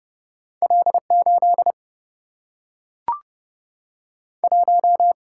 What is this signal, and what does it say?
Morse code, heard as a keyed tone, about 30 wpm, 700 Hz. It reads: L8 1
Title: Single-Letter Single-Number 30wpm